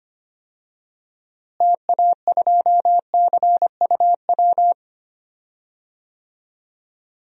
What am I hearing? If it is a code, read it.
Morse code, heard as a keyed tone, about 25 wpm, 700 Hz. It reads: TA2CUW
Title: DXCC-100 Call Signs - 6 Characters - Repeated 2x - 25wpm